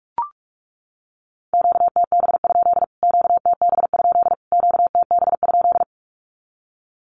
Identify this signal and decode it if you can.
Morse code, heard as a keyed tone, about 44 wpm, 700 Hz. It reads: QTB? QTB? QTB?
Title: Q-Codes Repeated 3x 45wpm